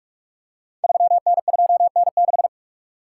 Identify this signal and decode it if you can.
Morse code, heard as a keyed tone, about 44 wpm, 690 Hz. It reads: 3N2N6
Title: Code Groups - 5 Letter:Numbers - 45wpm